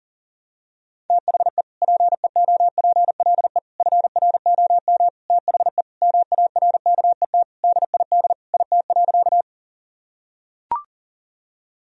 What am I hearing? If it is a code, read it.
Morse code, heard as a keyed tone, about 40 wpm, 700 Hz. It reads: THE PEOPLE FROM THE MARKET DID IT.
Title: Sentences from Top 1000 Words - Review Part 3 - 40wpm